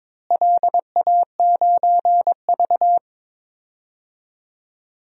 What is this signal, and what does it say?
Morse code, heard as a keyed tone, about 22 wpm, 700 Hz. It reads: LA9V
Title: DXCC-100 Call Signs - 4 Characters - Repeated 3x - 22wpm